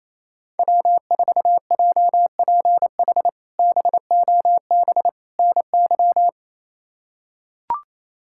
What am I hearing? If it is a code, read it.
Morse code, heard as a keyed tone, about 28 wpm, 700 Hz. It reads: W4JPH BOB NY
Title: K1USN SST QSO Practice - 28wpm